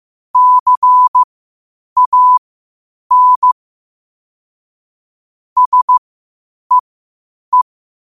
Morse code, heard as a keyed tone, about 15 wpm, spaced as for 8 wpm, 1000 Hz. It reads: CAN SEE
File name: qotdfiles / 8wpm